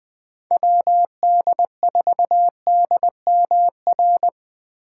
Morse code, heard as a keyed tone, about 20 wpm, 700 Hz. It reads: WD4DMR
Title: Speed Racing - USA 2x3 Call Signs 17wpm